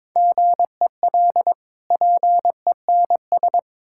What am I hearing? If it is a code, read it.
Morse code, heard as a keyed tone, about 22 wpm, 690 Hz. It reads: GEL PENS
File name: Difficult Words with Hints - 22wpm